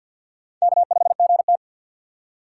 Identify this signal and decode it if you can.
Morse code, heard as a keyed tone, about 50 wpm, 700 Hz. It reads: K5GT
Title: USA Call Signs - 4 Characters - 2x Character Spacing - 50wpm